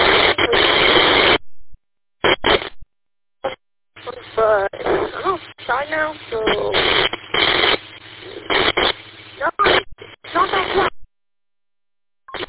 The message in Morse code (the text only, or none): none